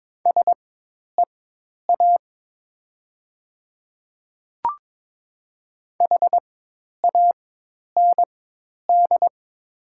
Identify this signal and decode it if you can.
Morse code, heard as a keyed tone, about 22 wpm, spaced as for 10 wpm, 700 Hz. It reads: SEA HAND
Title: 4x Character Spacing - Top 300 Words - 22wpm